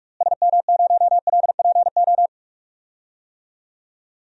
Morse code, heard as a keyed tone, about 46 wpm, 700 Hz. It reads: SM0LPO
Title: DXCC-100 Call Signs - 6 Characters - Repeated 2x - 45wpm